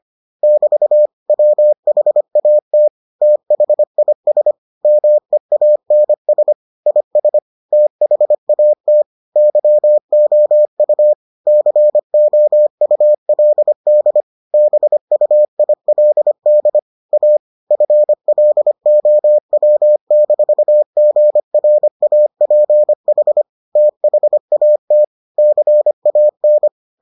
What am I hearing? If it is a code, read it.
Morse code, heard as a keyed tone, about 25 wpm, 600 Hz. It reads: <BT> WHAT THIS MEANS IS THAT YOU COULD BUILD A FLOW-GRAPH THAT CAN